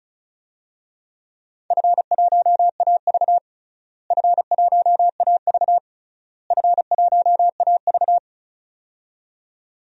Morse code, heard as a keyed tone, about 36 wpm, 700 Hz. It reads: F1AV F1AV F1AV
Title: DXCC-100 Call Signs - 4 Characters - Repeated 3x - 35wpm